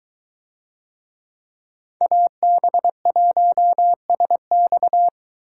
Morse code, heard as a keyed tone, about 22 wpm, 700 Hz. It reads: AB1SX